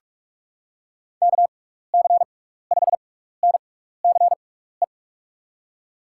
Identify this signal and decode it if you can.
Morse code, heard as a keyed tone, about 44 wpm, spaced as for 15 wpm, 700 Hz. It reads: KC5NCE
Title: USA Call Signs - 6 Characters - 6x Character Spacing - 45wpm